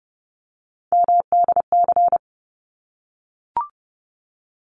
Morse code, heard as a keyed tone, about 30 wpm, 700 Hz. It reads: MDC
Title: ARRL Sections 30wpm